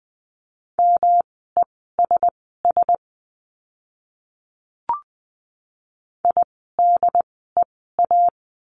Morse code, heard as a keyed tone, about 20 wpm, spaced as for 15 wpm, 700 Hz. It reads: MESS IDEA
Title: 108 Lesson 18f - Letter m - Words 20wpm